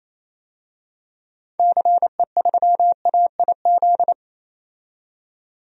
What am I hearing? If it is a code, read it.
Morse code, heard as a keyed tone, about 28 wpm, 700 Hz. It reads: CE3AIZ
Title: DXCC-100 Call Signs - 6 Characters - 28wpm